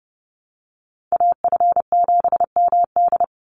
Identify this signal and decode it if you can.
Morse code, heard as a keyed tone, about 30 wpm, 700 Hz. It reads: AF7MD
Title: NA SOTA Chaser Callsigns 30wpm